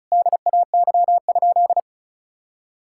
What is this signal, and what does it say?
Morse code, heard as a keyed tone, about 34 wpm, 700 Hz. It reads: DAY?
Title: Sentences from Top 1600 Words 35wpm